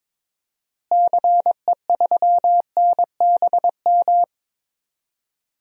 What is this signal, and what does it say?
Morse code, heard as a keyed tone, about 21 wpm, 700 Hz. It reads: CE3NBM
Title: DXCC-100 Call Signs - 6 Characters - Repeated 3x - 22wpm